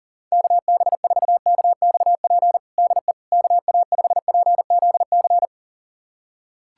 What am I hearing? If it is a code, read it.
Morse code, heard as a keyed tone, about 37 wpm, 700 Hz. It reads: KB4KXP DE KA5PZC
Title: USA Callsign DE Callsign - 6 Characters - 40wpm